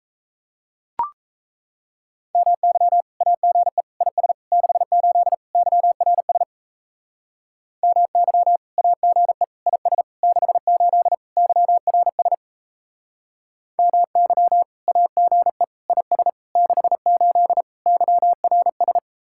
Morse code, heard as a keyed tone, about 34 wpm, 700 Hz. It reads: MY AGE IS 68 YRS MY AGE IS 68 YRS MY AGE IS 68 YRS
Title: Speed-Racing Common QSO Phrases - 28wpm